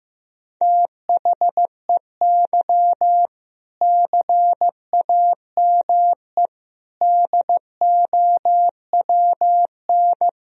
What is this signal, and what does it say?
Morse code, heard as a keyed tone, about 16 wpm, 700 Hz. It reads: THEY CAME DOWN
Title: Sentences from Top 1000 Words - Review Part 1 - 15wpm